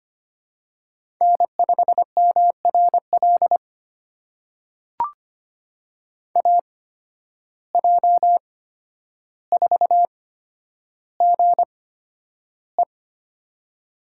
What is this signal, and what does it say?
Morse code, heard as a keyed tone, about 25 wpm, spaced as for 11 wpm, 700 Hz. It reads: N5MRL AJ4GE
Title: USA Call Signs - 5 Characters - 8x Character Spacing - 25wpm